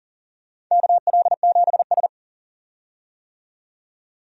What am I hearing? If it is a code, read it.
Morse code, heard as a keyed tone, about 40 wpm, 700 Hz. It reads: KL7S
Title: DXCC-100 Call Signs - 4 Characters - 40wpm